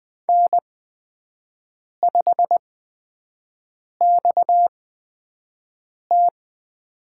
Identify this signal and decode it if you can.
Morse code, heard as a keyed tone, about 20 wpm, spaced as for 5 wpm, 700 Hz. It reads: N5XT
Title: USA Call Signs - 8x Character Spacing - 20wpm